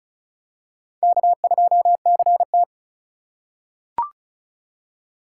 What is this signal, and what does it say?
Morse code, heard as a keyed tone, about 34 wpm, 700 Hz. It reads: K2CT
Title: USA Call Signs - 4 Characters - 8x Character Spacing - 35wpm